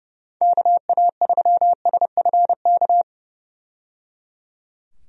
Morse code, heard as a keyed tone, about 32 wpm, 700 Hz. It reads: KA3SFK
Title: USA Call Signs - 6 Characters - 30wpm